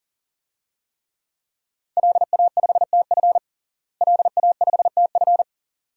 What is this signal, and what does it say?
Morse code, heard as a keyed tone, about 40 wpm, 700 Hz. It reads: LA5TF LA5TF
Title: DXCC-100 Call Signs - 5 Characters - Repeated 2x - 40wpm